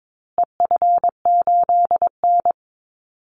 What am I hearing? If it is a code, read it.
Morse code, heard as a keyed tone, about 22 wpm, 700 Hz. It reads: EF8N
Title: Call Signs Repeated 3x - 22wpm